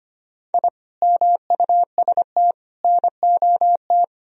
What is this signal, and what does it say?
Morse code, heard as a keyed tone, about 25 wpm, 700 Hz. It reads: I MUST NOT
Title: Sentences from Top 1000 Words - Review Part 2 - 25wpm